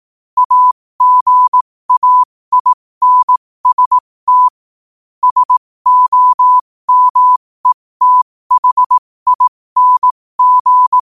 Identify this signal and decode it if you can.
Morse code, heard as a keyed tone, about 18 wpm, spaced as for 15 wpm, 1000 Hz. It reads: AGAINST SOMETHING
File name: qotdfiles / 15wpm